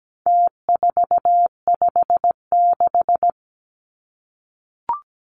Morse code, heard as a keyed tone, about 17 wpm, 700 Hz. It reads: T456
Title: Numbers - Contest Style - 4-Digit 17wpm